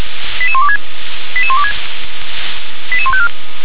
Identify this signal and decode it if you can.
Five-tone selective call (ZVEI), heard as quick runs of five tones, tones of about 70 ms. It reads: 88126 88126 88154